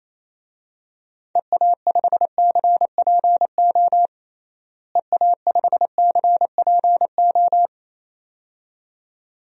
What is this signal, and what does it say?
Morse code, heard as a keyed tone, about 28 wpm, 700 Hz. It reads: EA5CPO EA5CPO
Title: DXCC-100 Call Signs - 6 Characters - Repeated 2x - 28wpm